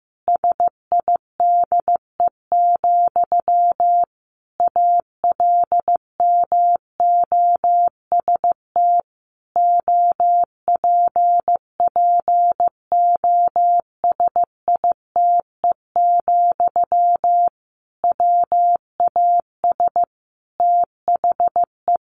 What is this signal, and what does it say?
Morse code, heard as a keyed tone, about 15 wpm, 700 Hz. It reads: SIDE, ALMOST OPPOSITE, WAS THE
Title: Sentences from Top 1300 Words 15wpm